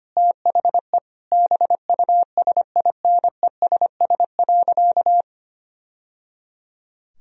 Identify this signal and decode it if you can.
Morse code, heard as a keyed tone, about 25 wpm, 700 Hz. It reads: THE BUSINESS.
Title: Sentences from Top 1000 Words - Review Part 3 - 25wpm